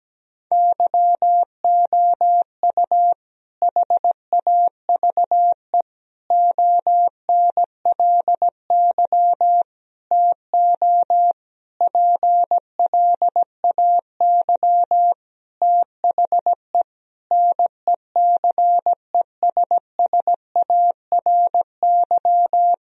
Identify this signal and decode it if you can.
Morse code, heard as a keyed tone, about 17 wpm, 700 Hz. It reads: YOU HAVE ONLY TO PLAY THE NECESSARY
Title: Sentences from Top 1000 Words - Review Part 1 - 17wpm